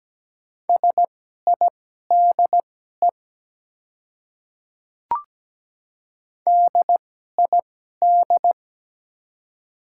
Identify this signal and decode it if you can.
Morse code, heard as a keyed tone, about 17 wpm, spaced as for 12 wpm, 690 Hz. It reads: SIDE DID
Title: 2x Character Spacing - Top 100 Words - 17wpm